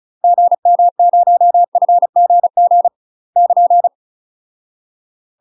Morse code, heard as a keyed tone, about 36 wpm, 700 Hz. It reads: GM0FGG <KN>
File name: UK MASTER QSOs TRAINING / 35WPM / UK MASTER QSOs 0043